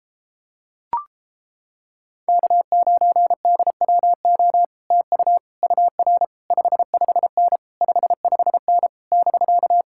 Morse code, heard as a keyed tone, about 31 wpm, 700 Hz. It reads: K9DWO TU UR 55N 55N <BK>